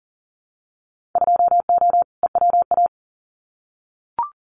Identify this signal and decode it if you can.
Morse code, heard as a keyed tone, about 40 wpm, 700 Hz. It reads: 2O EWA
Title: Winter Field Day Exchanges Encore 40wpm